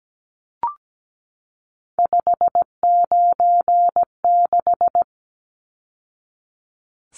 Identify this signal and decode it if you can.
Morse code, heard as a keyed tone, about 17 wpm, 700 Hz. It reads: 596
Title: Numbers - 3-Digit 17wpm